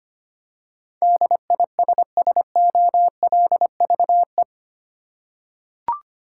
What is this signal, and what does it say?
Morse code, heard as a keyed tone, about 25 wpm, 700 Hz. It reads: DISSOLVE
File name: 8 Letter Words - First 3 Letters Given - 25wpm